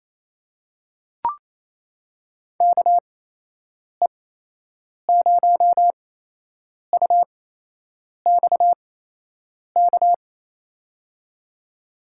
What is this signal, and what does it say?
Morse code, heard as a keyed tone, about 27 wpm, spaced as for 8 wpm, 700 Hz. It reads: KE0UXK
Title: USA Call Signs - 6 Characters - 8x Character Spacing - 28wpm